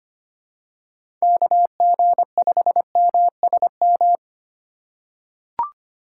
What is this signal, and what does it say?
Morse code, heard as a keyed tone, about 25 wpm, 700 Hz. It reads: KG5MSM
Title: USA Call Signs - 6 Characters - 3x Character Spacing - 25wpm